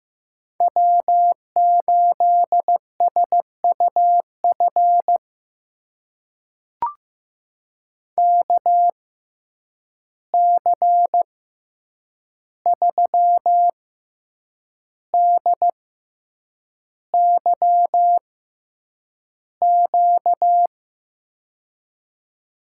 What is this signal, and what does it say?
Morse code, heard as a keyed tone, about 15 wpm, spaced as for 7 wpm, 700 Hz. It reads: W8SUF KC3DYQ
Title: USA Call Signs - 6x Character Spacing - 15wpm